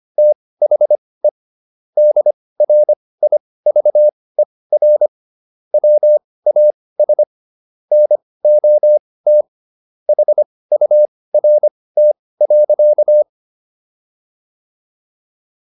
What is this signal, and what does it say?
Morse code, heard as a keyed tone, about 25 wpm, spaced as for 18 wpm, 600 Hz. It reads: THE DRIVER WAS NOT HURT.